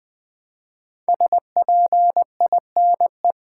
Morse code, heard as a keyed tone, about 20 wpm, 700 Hz. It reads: SPINE
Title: Build A Word - 1 Letter At A Time - 5-Letter Words - 20wpm